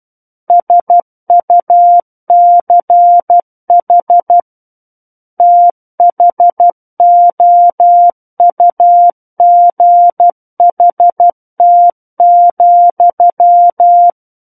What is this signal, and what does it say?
Morse code, heard as a keyed tone, about 12 wpm, 700 Hz. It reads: SUCH THOUGHT,
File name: Book One / 03-Chapter 3 - 12 WPM